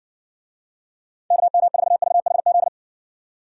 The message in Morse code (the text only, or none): KM4VHZ